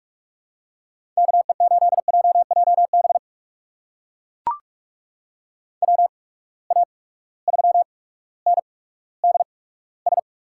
Morse code, heard as a keyed tone, about 42 wpm, spaced as for 19 wpm, 700 Hz. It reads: KE8JJB WA3NDS
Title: USA Call Signs - 6 Characters - 8x Character Spacing - 45wpm